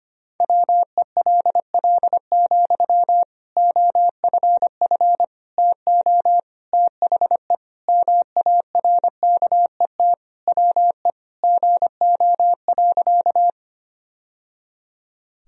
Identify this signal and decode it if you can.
Morse code, heard as a keyed tone, about 25 wpm, 700 Hz. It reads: WELL, OFF TO THE MARKET WE GO.
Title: Sentences from Top 1000 Words - Review Part 3 - 25wpm